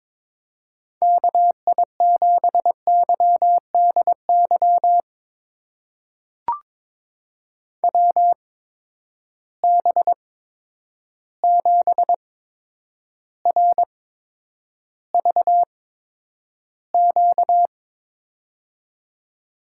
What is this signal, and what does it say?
Morse code, heard as a keyed tone, about 22 wpm, spaced as for 9 wpm, 700 Hz. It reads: KI7YDY WB7RVQ
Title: USA Call Signs - 6 Characters - 8x Character Spacing - 22wpm